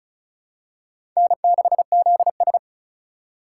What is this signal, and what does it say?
Morse code, heard as a keyed tone, about 36 wpm, 700 Hz. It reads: N6ZS
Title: USA Call Signs - 4x Character Spacing - 35wpm